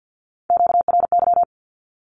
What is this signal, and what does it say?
Morse code, heard as a keyed tone, about 50 wpm, 700 Hz. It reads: QRY
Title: Q-Codes Repeated 3x 50wpm